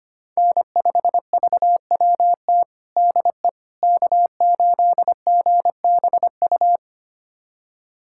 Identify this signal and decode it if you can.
Morse code, heard as a keyed tone, about 25 wpm, 700 Hz. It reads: N5VWT DE K8GBU
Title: USA Callsign DE Callsign - 5 Characters - 25wpm